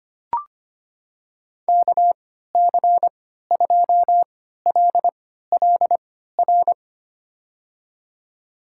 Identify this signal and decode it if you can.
Morse code, heard as a keyed tone, about 25 wpm, spaced as for 14 wpm, 700 Hz. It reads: KC2LLR
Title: USA Call Signs - 6 Characters - 3x Character Spacing - 25wpm